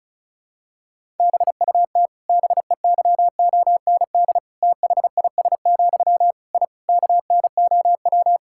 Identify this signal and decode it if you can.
Morse code, heard as a keyed tone, about 34 wpm, 700 Hz. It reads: BUT BEYOND THIS, I KNOW
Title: Sentences from Top 1000 Words - Review Part 6 - 35wpm